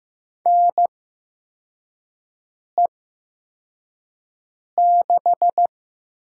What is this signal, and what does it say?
Morse code, heard as a keyed tone, about 15 wpm, spaced as for 4 wpm, 700 Hz. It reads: NE6
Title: USA Call Signs - 4 Characters - 8x Character Spacing - 15wpm